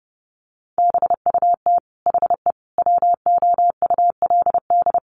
Morse code, heard as a keyed tone, about 28 wpm, 700 Hz. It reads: BUT HE WOULD
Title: Sentences from Top 1100 Words 30wpm